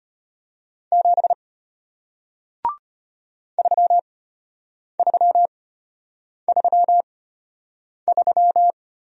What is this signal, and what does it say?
Morse code, heard as a keyed tone, about 33 wpm, spaced as for 8 wpm, 700 Hz. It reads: 7 3333